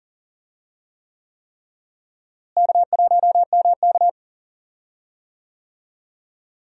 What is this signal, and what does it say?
Morse code, heard as a keyed tone, about 40 wpm, 700 Hz. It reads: K1MK
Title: USA Call Signs - 4 Characters - Repeated 3x - 40wpm